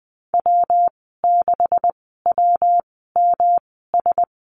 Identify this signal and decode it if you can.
Morse code, heard as a keyed tone, about 20 wpm, spaced as for 14 wpm, 700 Hz. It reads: W6WMS
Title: 140 Lesson 23g - Letter f - Call Signs 20wpm